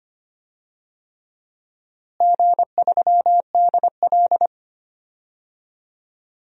Morse code, heard as a keyed tone, about 25 wpm, 700 Hz. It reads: G3DL
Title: DXCC-100 Call Signs - 4 Characters - Repeated 3x - 25wpm